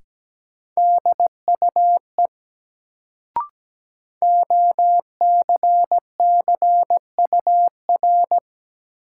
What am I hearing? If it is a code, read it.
Morse code, heard as a keyed tone, about 17 wpm, 700 Hz. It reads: DUE OCCUR
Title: Sentences from Top 900 Words - New Word Drill - 17wpm